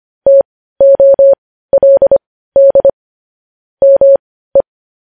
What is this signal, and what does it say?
Morse code, heard as a keyed tone, about 25 wpm, spaced as for 15 wpm, 550 Hz. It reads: TOLD ME
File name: podcast / News headlines-1x-25x15